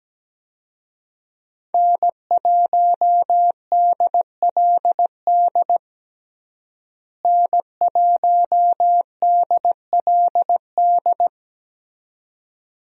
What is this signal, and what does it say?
Morse code, heard as a keyed tone, about 17 wpm, 700 Hz. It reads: N1DLD N1DLD
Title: USA Call Signs - 5 Characters - Repeated 2x - 17wpm